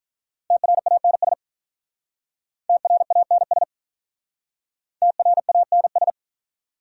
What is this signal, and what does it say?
Morse code, heard as a keyed tone, about 44 wpm, 690 Hz. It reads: TRANS TRANS TRANS
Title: Speed-Racing Prefixes and Suffixes 35wpm